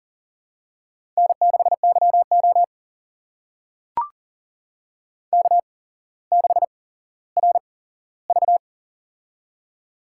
Morse code, heard as a keyed tone, about 41 wpm, spaced as for 17 wpm, 700 Hz. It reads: N6YO K6RV
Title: USA Call Signs - 4 Characters - 8x Character Spacing - 40wpm